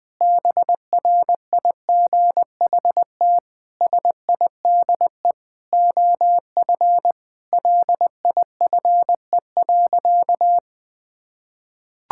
Morse code, heard as a keyed tone, about 20 wpm, 700 Hz. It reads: BRIGHT SIDE OF LIFE.
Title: Sentences with 8-Word Phrases - 20wpm